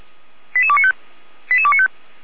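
Five-tone selective call (ZVEI), heard as quick runs of five tones, tones of about 70 ms. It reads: 88295 88295
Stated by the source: ZVEI five-tone selective call received on SDR, address 88136.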